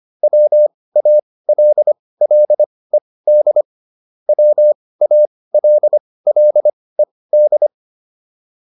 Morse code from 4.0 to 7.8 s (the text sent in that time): WALLED